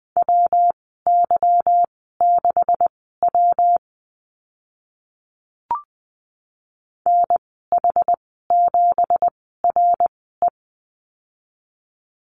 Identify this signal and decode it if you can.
Morse code, heard as a keyed tone, about 20 wpm, spaced as for 14 wpm, 700 Hz. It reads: WY6W NH7RE
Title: 167 Lesson 27g - Number 7 - Call Signs 20wpm